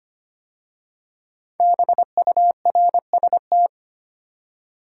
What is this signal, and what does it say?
Morse code, heard as a keyed tone, about 25 wpm, 700 Hz. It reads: BURST
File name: 5 Letter Words - First 3 Letters Given - 25wpm